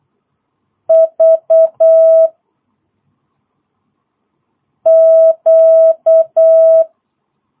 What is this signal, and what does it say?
Morse code, heard as a keyed tone, about 8 wpm, 650 Hz. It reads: V Q
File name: jh8cbh / 2022cw